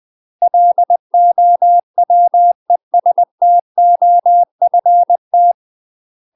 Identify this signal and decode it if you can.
Morse code, heard as a keyed tone, about 20 wpm, 700 Hz. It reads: LOWESTOFT
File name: UK MASTER QSOs TRAINING / 20WPM / UK MASTER QSOs 0049